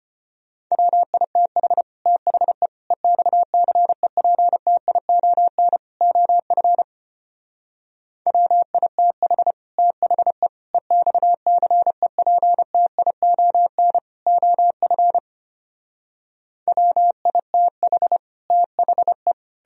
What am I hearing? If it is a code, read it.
Morse code, heard as a keyed tone, about 33 wpm, 700 Hz. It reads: WITH THE EXCEPTION OF WITH THE EXCEPTION OF WITH THE